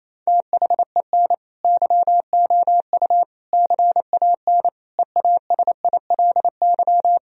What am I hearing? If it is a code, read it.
Morse code, heard as a keyed tone, about 28 wpm, 700 Hz. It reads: THEN YOU CAN EASILY
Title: Sentences from Top 1000 Words - Review Part 1 - 28wpm